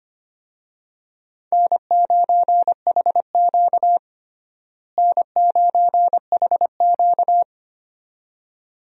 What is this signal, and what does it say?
Morse code, heard as a keyed tone, about 25 wpm, 700 Hz. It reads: N9HQ N9HQ
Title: USA Call Signs - 4 Characters - Repeated 2x - 25wpm